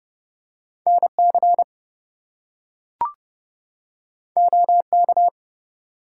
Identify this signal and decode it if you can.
Morse code, heard as a keyed tone, about 30 wpm, 700 Hz. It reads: NC OK
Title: US States and Canadian Provinces 30wpm